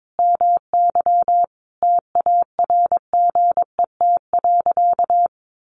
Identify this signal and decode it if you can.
Morse code, heard as a keyed tone, about 22 wpm, 700 Hz. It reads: MY TARGET.
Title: Sentences from Top 1200 Words 22wpm